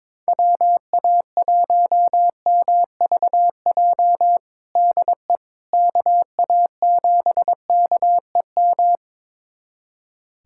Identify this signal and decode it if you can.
Morse code, heard as a keyed tone, about 22 wpm, 700 Hz. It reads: WA1MVJ DE KA7KEM
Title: USA Callsign DE Callsign - 6 Characters - 22wpm